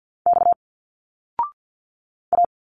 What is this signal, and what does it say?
Morse code, heard as a keyed tone, about 50 wpm, 700 Hz. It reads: X A
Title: Single Letters 50wpm